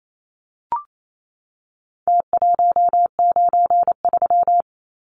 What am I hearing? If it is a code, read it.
Morse code, heard as a keyed tone, about 28 wpm, 700 Hz. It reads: T193
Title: Numbers - Contest Style - 4-Digit 28wpm